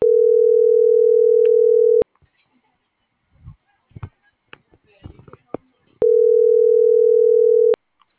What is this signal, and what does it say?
Ringback tone (North America), 440 + 480 Hz.